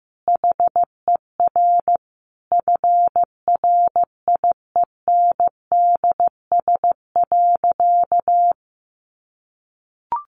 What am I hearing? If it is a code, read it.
Morse code, heard as a keyed tone, about 15 wpm, 700 Hz. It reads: HER FRIENDS.
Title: Easy Phrases 15wpm